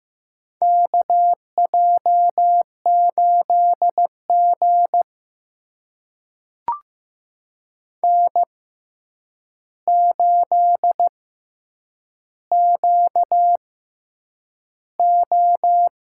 Morse code, heard as a keyed tone, about 15 wpm, spaced as for 8 wpm, 700 Hz. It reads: KJ8G N8QO
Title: USA Call Signs - 4 Characters - 6x Character Spacing - 15wpm